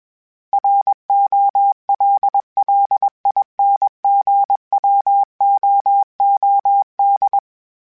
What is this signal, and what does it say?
Morse code, heard as a keyed tone, about 21 wpm, 800 Hz. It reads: ROLLINGWOOD
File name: fromSMs / FromSMs mp3s / Cities@20